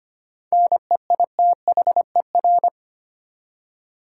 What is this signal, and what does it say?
Morse code, heard as a keyed tone, about 25 wpm, 700 Hz. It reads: NEITHER
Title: Seven Letter Words - 25wpm